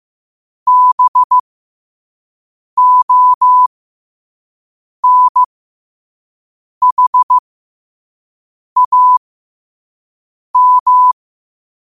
Morse code, heard as a keyed tone, about 15 wpm, spaced as for 5 wpm, 1000 Hz. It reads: BONHAM
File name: qotdfiles / 5wpm